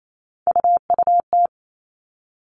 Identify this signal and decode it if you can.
Morse code, heard as a keyed tone, about 28 wpm, 700 Hz. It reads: UUT
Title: All Cut Numbers - 3-Digit 28wpm